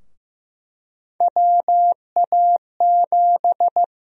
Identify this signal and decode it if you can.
Morse code, heard as a keyed tone, about 15 wpm, 700 Hz. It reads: WA7
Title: USA Call Sign Prefixes - Repeated 2x - 15wpm